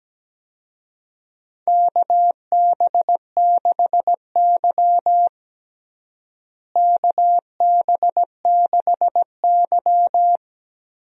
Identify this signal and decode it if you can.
Morse code, heard as a keyed tone, about 17 wpm, 700 Hz. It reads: KB6Y KB6Y
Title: USA Call Signs - 4 Characters - Repeated 3x - 17wpm